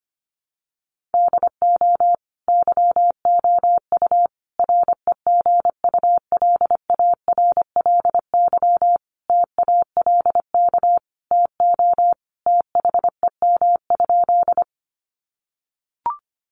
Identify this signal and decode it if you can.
Morse code, heard as a keyed tone, about 25 wpm, 700 Hz. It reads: DO YOU REGULARLY TALK TO THEM?
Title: Sentences from Top 1400 Words 25wpm